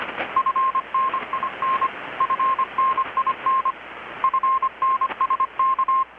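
Morse code, heard as a keyed tone, about 25 wpm, 1060 Hz, with radio noise. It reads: FNIN FNIN FNSK
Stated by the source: Morse from KPH's Night of Nights XX maritime radiotelegraphy event, received in Chiapas, Mexico about 3,800 km away on a Kenwood R-600 with a grounded 10 m long-wire antenna.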